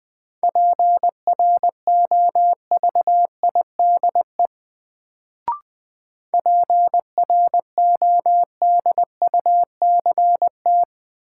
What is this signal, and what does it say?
Morse code, heard as a keyed tone, about 20 wpm, 700 Hz. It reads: PROVIDE PRODUCT
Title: Sentences from Top 1000 Words - Review Part 2 - New Word Drill - 20wpm